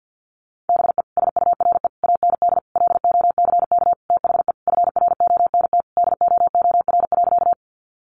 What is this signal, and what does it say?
Morse code, heard as a keyed tone, about 50 wpm, 700 Hz. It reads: BE SURE AND LOCK THE FRONT DOOR.
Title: Sentences from Top 1300 Words 50wpm